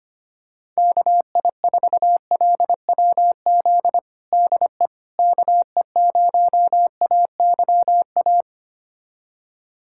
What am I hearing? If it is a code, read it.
Morse code, heard as a keyed tone, about 25 wpm, 700 Hz. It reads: KI4LWZ DE KE0AYA
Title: USA Callsign DE Callsign - 6 Characters - 25wpm